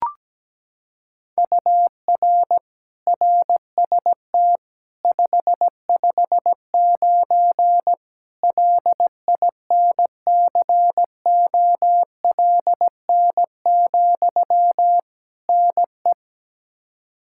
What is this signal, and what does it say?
Morse code, heard as a keyed tone, about 17 wpm, 700 Hz. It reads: UR RST 559 LINCOLN, NE
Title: Practice QSO - Exchange 1 - Beginning - 17wpm